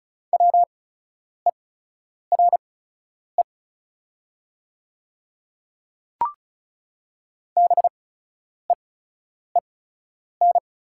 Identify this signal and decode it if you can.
Morse code, heard as a keyed tone, about 36 wpm, spaced as for 10 wpm, 700 Hz. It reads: WERE BEEN
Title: 8x Character Spacing - Top 100 Words - 35wpm